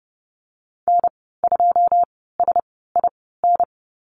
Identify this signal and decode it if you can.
Morse code, heard as a keyed tone, about 30 wpm, spaced as for 17 wpm, 700 Hz. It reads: N2SIN
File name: Lesson 14g - Number 2 - Call Signs 30wpm